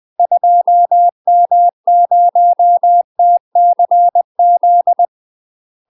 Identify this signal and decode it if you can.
Morse code, heard as a keyed tone, about 20 wpm, 700 Hz. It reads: 2M0TCZ
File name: UK MASTER QSOs TRAINING / 20WPM / UK MASTER QSOs 0069